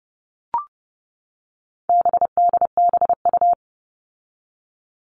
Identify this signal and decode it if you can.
Morse code, heard as a keyed tone, about 30 wpm, 700 Hz. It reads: BDBU
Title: All Cut Numbers - 4-Digit 30wpm